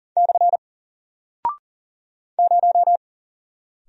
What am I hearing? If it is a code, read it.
Morse code, heard as a keyed tone, about 40 wpm, 700 Hz. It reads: / 0
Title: Single Character (Letter:Number:Punctuation:Prosign) Alt 40wpm